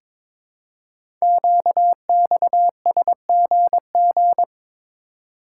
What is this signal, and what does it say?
Morse code, heard as a keyed tone, about 22 wpm, 700 Hz. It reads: QXSGG
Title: Code Groups - 5 Letter:Numbers - 22wpm